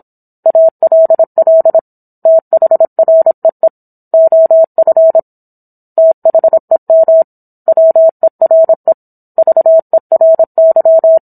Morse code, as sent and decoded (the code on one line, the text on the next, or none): .- .-.. .-.. / - .... .-. . / --- ..-. / - .... -- / .-- . .-. / ...- . .-. -.--
ALL THREE OF THEM WERE VERY